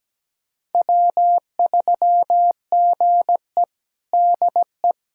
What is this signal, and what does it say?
Morse code, heard as a keyed tone, about 17 wpm, 700 Hz. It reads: W3GE DE
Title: USA Callsign DE Callsign - 4 Characters - 17wpm